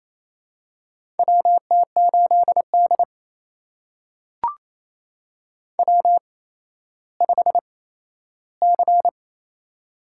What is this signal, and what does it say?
Morse code, heard as a keyed tone, about 27 wpm, spaced as for 13 wpm, 700 Hz. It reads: WT8D W5C
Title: USA Call Signs - 4 Characters - 8x Character Spacing - 28wpm